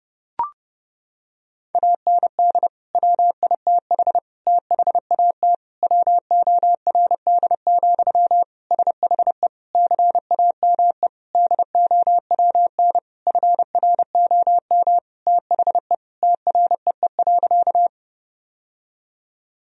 Morse code, heard as a keyed tone, about 32 wpm, 700 Hz. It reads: AND WITH THAT WORD, SHE CAME DOWN FROM THE TREE.
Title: Sentences from Top 1000 Words - Review Part 1 - 30wpm